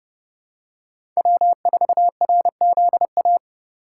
Morse code, heard as a keyed tone, about 30 wpm, 700 Hz. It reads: W4RZA
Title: USA Call Signs - 3x Character Spacing - 30wpm